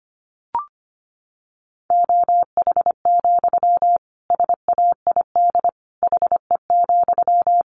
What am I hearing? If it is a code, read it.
Morse code, heard as a keyed tone, about 25 wpm, 700 Hz. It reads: OH, SAID HE,
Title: Sentences from Top 500 Words 25wpm